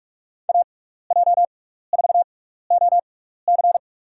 Morse code, heard as a keyed tone, about 44 wpm, spaced as for 15 wpm, 690 Hz. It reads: AJ4OC
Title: USA Call Signs - 5 Characters - 6x Character Spacing - 45wpm